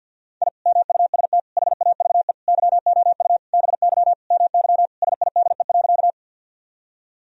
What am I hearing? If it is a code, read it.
Morse code, heard as a keyed tone, about 50 wpm, 700 Hz. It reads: I MUST HAVE YOU BY MY SIDE.
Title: Sentences from Top 1000 Words - Review Part 1 - 50wpm